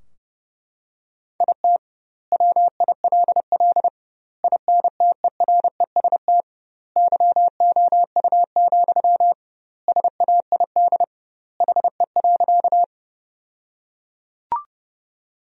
Morse code, heard as a keyed tone, about 30 wpm, 700 Hz. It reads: IT WILL INTEREST YOU, SAID HE.
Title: 2x Word-Spacing - Sentences from Top 400 Words - 30wpm